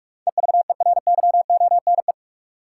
Morse code, heard as a keyed tone, about 42 wpm, 700 Hz. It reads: EVERYONE
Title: Sentences from Top 800 Words - New Word Drill - 45wpm